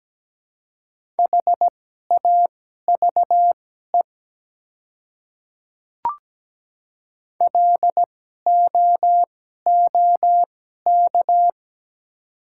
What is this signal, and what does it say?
Morse code, heard as a keyed tone, about 17 wpm, spaced as for 12 wpm, 690 Hz. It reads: HAVE LOOK
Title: 2x Character Spacing - Top 100 Words - 17wpm